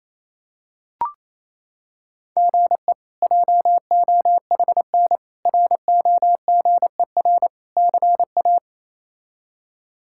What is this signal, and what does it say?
Morse code, heard as a keyed tone, about 28 wpm, 700 Hz. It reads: GE JOHN ROGER CA
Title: K1USN SST QSO Practice - 28wpm